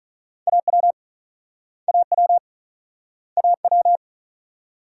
Morse code, heard as a keyed tone, about 46 wpm, 700 Hz. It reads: AW AW AW